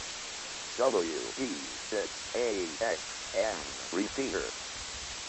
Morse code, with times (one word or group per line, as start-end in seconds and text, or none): none